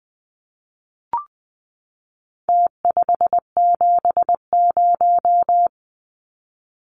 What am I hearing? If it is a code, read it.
Morse code, heard as a keyed tone, about 20 wpm, 700 Hz. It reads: T570
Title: Numbers - Contest Style - 4-Digit 20wpm